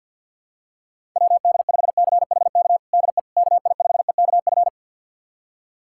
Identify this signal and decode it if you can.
Morse code, heard as a keyed tone, about 52 wpm, 700 Hz. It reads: WD5CHK DE KI5EKF
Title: USA Callsign DE Callsign 50wpm